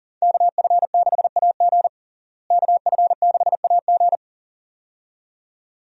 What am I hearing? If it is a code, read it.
Morse code, heard as a keyed tone, about 40 wpm, 700 Hz. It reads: KF6AG KF6AG
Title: USA Call Signs - 5 Characters - Repeated 2x - 40wpm